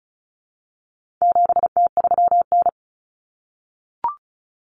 Morse code, heard as a keyed tone, about 34 wpm, 700 Hz. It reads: 7T3N
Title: Common Cut Numbers - 4-Digit 35wpm